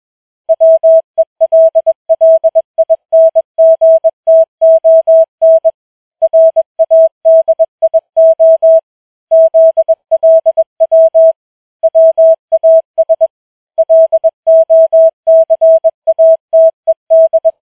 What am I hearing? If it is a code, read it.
Morse code, heard as a keyed tone, about 21 wpm, 650 Hz. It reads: WELLINGTON RADIO ZLW WAS LOCATED